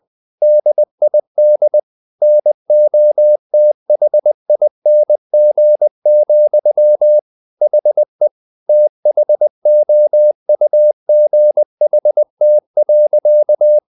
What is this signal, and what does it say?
Morse code, heard as a keyed tone, about 20 wpm, 600 Hz. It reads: DID NOTHING, HE THOUGHT.
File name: the old man and the sea / the old man and the sea part 1 20wpm